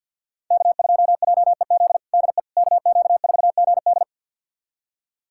Morse code, heard as a keyed tone, about 51 wpm, 700 Hz. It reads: K2JEZ DE KQ4GD
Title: USA Callsign DE Callsign - 5 Characters - 50wpm